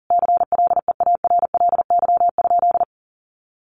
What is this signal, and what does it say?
Morse code, heard as a keyed tone, about 40 wpm, 700 Hz. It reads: CLEARLY?
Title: Sentences from Top 1300 Words 40wpm